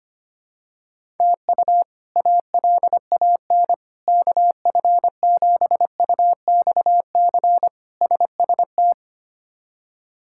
Morse code, heard as a keyed tone, about 25 wpm, 700 Hz. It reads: TU ALAN KF7UXC SST
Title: K1USN SST QSO Practice - 25wpm